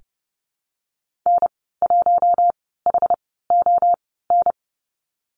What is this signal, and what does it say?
Morse code, heard as a keyed tone, about 30 wpm, spaced as for 17 wpm, 700 Hz. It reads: N1HON
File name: Lesson 14g - Number 2 - Call Signs 30wpm